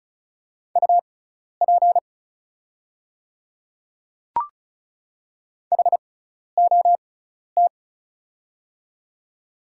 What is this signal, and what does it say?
Morse code, heard as a keyed tone, about 36 wpm, spaced as for 12 wpm, 690 Hz. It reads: UP HOT
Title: 6x Character Spacing - Top 100 Words - 35wpm